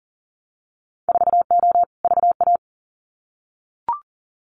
Morse code, heard as a keyed tone, about 41 wpm, 700 Hz. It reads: 4O VA
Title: Winter Field Day Exchanges Encore 40wpm